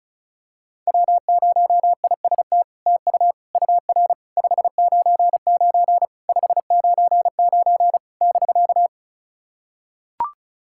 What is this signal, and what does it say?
Morse code, heard as a keyed tone, about 33 wpm, 700 Hz. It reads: W0IST TU UR 599 599 <BK>
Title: SOTA QSO Practice 35wpm